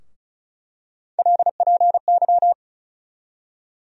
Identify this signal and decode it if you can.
Morse code, heard as a keyed tone, about 36 wpm, 700 Hz. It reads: LPY
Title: USA Call Sign Postfixes - 3 Letters 35wpm